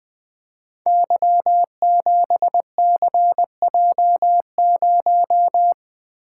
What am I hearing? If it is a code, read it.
Morse code, heard as a keyed tone, about 21 wpm, 700 Hz. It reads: Y7CJ0
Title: Code Groups - 5 Letter:Numbers - 20wpm